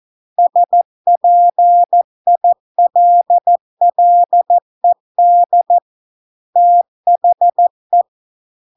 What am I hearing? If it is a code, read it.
Morse code, heard as a keyed tone, about 14 wpm, 700 Hz. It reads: SPILLED THE